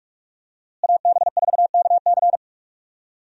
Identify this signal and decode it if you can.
Morse code, heard as a keyed tone, about 46 wpm, 700 Hz. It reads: AB4KC
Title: USA Call Signs - 5 Characters - Repeated 2x - 45wpm